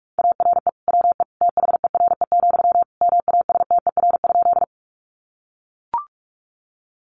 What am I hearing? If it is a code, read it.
Morse code, heard as a keyed tone, about 44 wpm, 700 Hz. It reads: ARE WE THERE, MASTER?
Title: Sentences from Top 1300 Words 45wpm